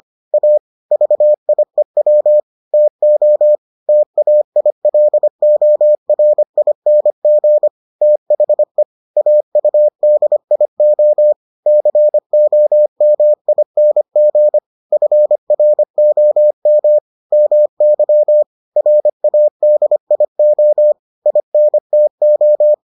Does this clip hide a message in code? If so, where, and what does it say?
Yes: Morse code from 0.3 to 22.7 s, heard as a keyed tone, about 25 wpm, 600 Hz. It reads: A VIEW TO TAILORING THE AUDIO COMING FROM MY RADIO INTO